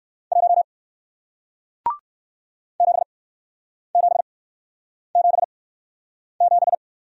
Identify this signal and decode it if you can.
Morse code, heard as a keyed tone, about 57 wpm, spaced as for 9 wpm, 700 Hz. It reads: 1 7777